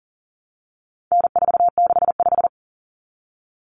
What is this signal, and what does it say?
Morse code, heard as a keyed tone, about 40 wpm, 700 Hz. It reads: N465
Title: Common Cut Numbers - 4-Digit 40wpm